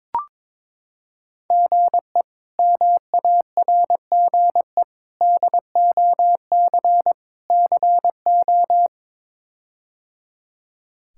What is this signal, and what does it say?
Morse code, heard as a keyed tone, about 21 wpm, 700 Hz. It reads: GE MARGE DOC CO
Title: K1USN SST QSO Practice - 22wpm